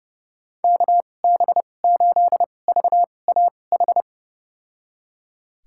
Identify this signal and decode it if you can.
Morse code, heard as a keyed tone, about 30 wpm, spaced as for 22 wpm, 700 Hz. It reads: KB8VAH
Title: USA Call Signs - 2x Character Spacing - 30wpm